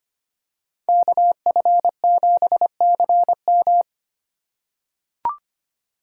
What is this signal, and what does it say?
Morse code, heard as a keyed tone, about 25 wpm, 700 Hz. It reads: KF7CM
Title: USA Call Signs - 5 Characters - 2x Character Spacing - 25wpm